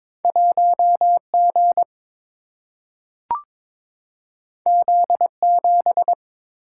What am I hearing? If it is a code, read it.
Morse code, heard as a keyed tone, about 22 wpm, 700 Hz. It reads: 1 G Z 7
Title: Single-Letter Single-Number 22wpm